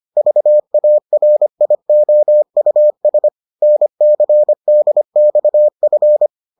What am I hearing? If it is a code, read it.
Morse code, heard as a keyed tone, about 25 wpm, 600 Hz. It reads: VARIOUS NCDXF